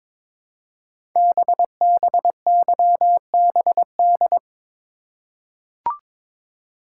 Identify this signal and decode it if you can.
Morse code, heard as a keyed tone, about 22 wpm, 690 Hz. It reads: BBYBD
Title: Code Groups - 5 Letters - 22wpm